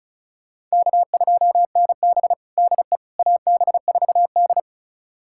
Morse code, heard as a keyed tone, about 36 wpm, 700 Hz. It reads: K2NB DE AB4D
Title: USA Callsign DE Callsign - 4 Characters - 35wpm